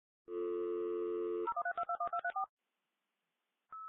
DTMF, heard as telephone touch tones, tones of about 90 ms. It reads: 013221234